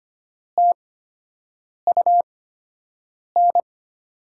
Morse code, heard as a keyed tone, about 25 wpm, spaced as for 7 wpm, 700 Hz. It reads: TUN
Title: USA Call Signs - 5 Characters - 8x Character Spacing - 25wpm